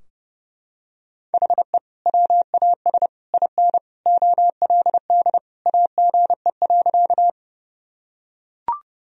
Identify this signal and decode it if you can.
Morse code, heard as a keyed tone, about 30 wpm, 700 Hz. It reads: HE WAS IN OLD AGE.
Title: Sentences from Top 1000 Words - Review Part 3 - 30wpm